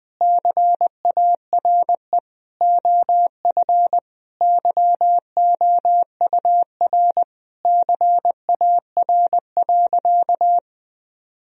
Morse code, heard as a keyed tone, about 21 wpm, 700 Hz. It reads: CARE OF YOUR CAR.
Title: Sentences from Top 1000 Words - Review Part 1 - 20wpm